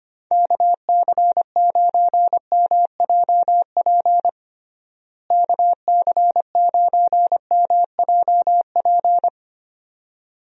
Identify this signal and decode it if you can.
Morse code, heard as a keyed tone, about 25 wpm, 700 Hz. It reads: KC9MJP KC9MJP
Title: USA Call Signs - 6 Characters - Repeated 2x - 25wpm